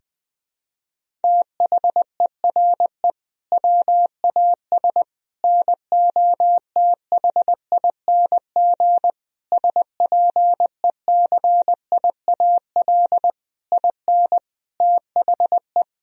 Morse code, heard as a keyed tone, about 20 wpm, 700 Hz. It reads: THERE WAS NOTHING SPECIAL IN THE
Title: Sentences from Top 1000 Words - Review Part 3 - 20wpm